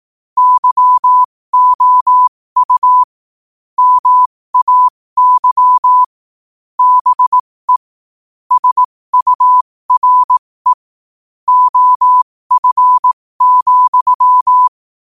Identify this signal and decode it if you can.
Morse code, heard as a keyed tone, about 18 wpm, spaced as for 15 wpm, 1000 Hz. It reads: YOU MAY BE SURE OF,
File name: qotdfiles / 15wpm